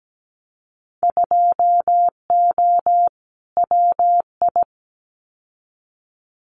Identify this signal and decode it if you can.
Morse code, heard as a keyed tone, about 17 wpm, 700 Hz. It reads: 2O WI
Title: Winter Field Day Exchanges 17wpm